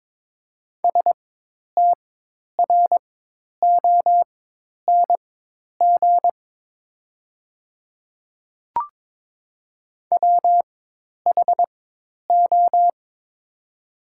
Morse code, heard as a keyed tone, about 22 wpm, spaced as for 10 wpm, 700 Hz. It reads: STRONG WHO
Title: 4x Character Spacing - Top 500 Words - 22wpm